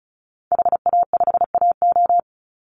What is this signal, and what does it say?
Morse code, heard as a keyed tone, about 34 wpm, 700 Hz. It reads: HA5AO
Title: Call Signs Repeated 3x - 35wpm